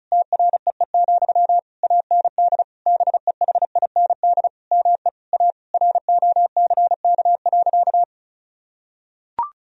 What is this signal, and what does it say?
Morse code, heard as a keyed tone, about 36 wpm, 700 Hz. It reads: TREE, AND BEHIND ME A ROCK.
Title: Sentences from Top 1000 Words - Review Part 2 - 35wpm